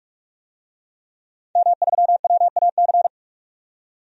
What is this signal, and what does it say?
Morse code, heard as a keyed tone, about 41 wpm, 700 Hz. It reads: M3WAC
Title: DXCC-100 Call Signs - 5 Characters - 45wpm